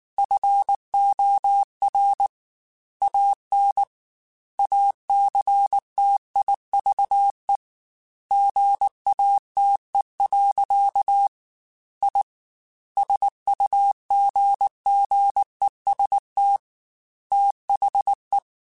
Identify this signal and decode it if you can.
Morse code, heard as a keyed tone, about 19 wpm, 800 Hz. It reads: FOR AN ACTIVE GATE. I SUGGEST THE
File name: Technical Subjects / Technical Subjects-mp3s / AND Gate @18